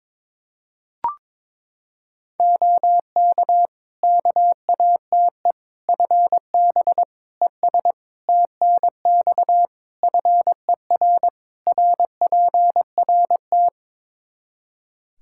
Morse code, heard as a keyed tone, about 22 wpm, 700 Hz. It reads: OK KATE FB ES TNX FER RPRT
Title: Practice QSO - Exchange 2 - Info - 22wpm